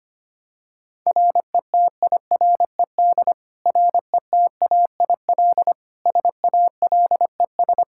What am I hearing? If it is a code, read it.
Morse code, heard as a keyed tone, about 25 wpm, 700 Hz. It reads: RETIRED RETAIL SALES
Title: Practice QSO - Exchange 3 - More Info - 25wpm